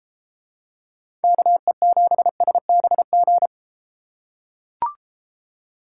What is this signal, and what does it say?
Morse code, heard as a keyed tone, about 31 wpm, 700 Hz. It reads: KE7SBG